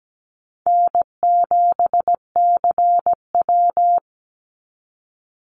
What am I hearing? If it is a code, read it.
Morse code, heard as a keyed tone, about 17 wpm, 700 Hz. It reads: N7CW
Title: NA SOTA Chaser Callsigns 17wpm